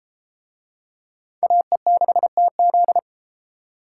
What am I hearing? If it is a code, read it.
Morse code, heard as a keyed tone, about 30 wpm, 700 Hz. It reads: AE6TZ